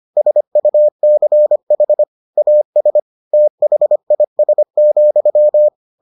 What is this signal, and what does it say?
Morse code, heard as a keyed tone, about 25 wpm, 600 Hz. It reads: SUCH AS THIS,